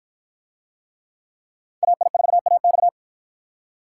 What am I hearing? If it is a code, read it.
Morse code, heard as a keyed tone, about 50 wpm, 700 Hz. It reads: AI4AX